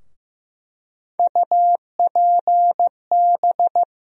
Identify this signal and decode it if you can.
Morse code, heard as a keyed tone, about 15 wpm, 700 Hz. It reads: UPB
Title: USA Call Sign Postfixes - 3 Letters 15wpm